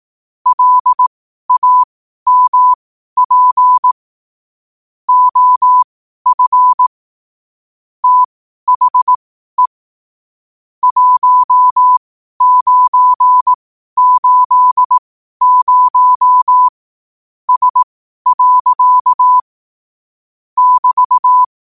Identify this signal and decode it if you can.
Morse code, heard as a keyed tone, about 18 wpm, spaced as for 12 wpm, 1000 Hz. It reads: LAMP OF THE 1980S. <BT>
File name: qotdfiles / 12wpm